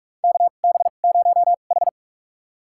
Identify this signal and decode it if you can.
Morse code, heard as a keyed tone, about 46 wpm, spaced as for 33 wpm, 700 Hz. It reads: KB0H
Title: USA Call Signs - 4 Characters - 2x Character Spacing - 45wpm